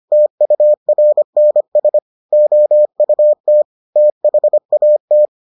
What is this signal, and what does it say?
Morse code, heard as a keyed tone, about 25 wpm, 600 Hz. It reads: TURNS OUT THAT